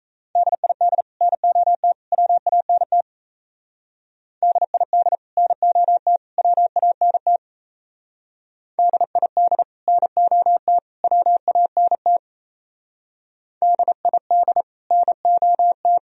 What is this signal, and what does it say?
Morse code, heard as a keyed tone, about 34 wpm, 700 Hz. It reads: DID NOT WANT DID NOT WANT DID NOT WANT DID NOT
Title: Speed-Racing - Sets of 3 Words 28wpm